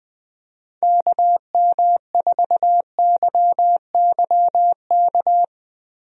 Morse code, heard as a keyed tone, about 21 wpm, 700 Hz. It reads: KM4YYK
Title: USA Call Signs - 6 Characters - Repeated 2x - 20wpm